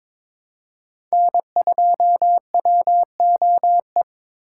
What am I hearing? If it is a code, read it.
Morse code, heard as a keyed tone, about 26 wpm, 700 Hz. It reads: N2WOE
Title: Speed Racing - USA Call Signs 22wpm